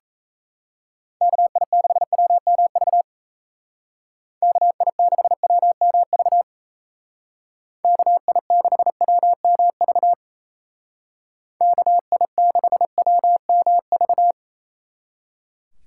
Morse code, heard as a keyed tone, about 36 wpm, 700 Hz. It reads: KI6WMV KI6WMV KI6WMV KI6WMV